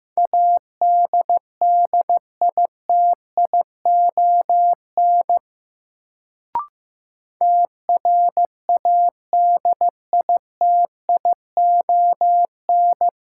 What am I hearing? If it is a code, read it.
Morse code, heard as a keyed tone, about 15 wpm, 700 Hz. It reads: ADDITION TRADITION
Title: Sentences from Top 1500 Words - New Word Drill Alt - 15wpm